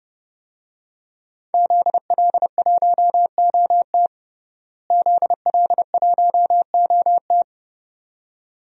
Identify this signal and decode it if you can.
Morse code, heard as a keyed tone, about 30 wpm, 700 Hz. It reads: ZL1OT ZL1OT
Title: DXCC-100 Call Signs - 5 Characters - Repeated 2x - 30wpm